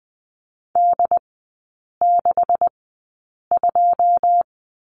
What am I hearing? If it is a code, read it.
Morse code, heard as a keyed tone, about 20 wpm, spaced as for 8 wpm, 700 Hz. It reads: D62
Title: 127 Lesson 21e - Number 6 - Three Characters 20wpm